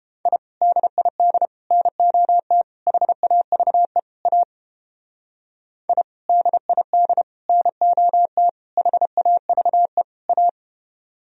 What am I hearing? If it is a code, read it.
Morse code, heard as a keyed tone, about 29 wpm, 700 Hz. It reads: I DID NOT HAVE A I DID NOT HAVE A